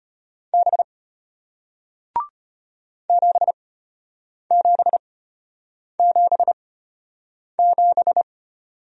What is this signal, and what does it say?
Morse code, heard as a keyed tone, about 33 wpm, spaced as for 8 wpm, 700 Hz. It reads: B 7777